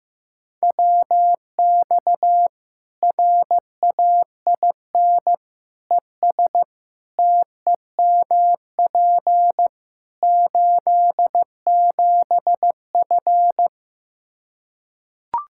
Morse code, heard as a keyed tone, about 15 wpm, 700 Hz. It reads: WX RAIN ES TEMP 87F
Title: Practice QSO - Exchange 2 - Info - 15wpm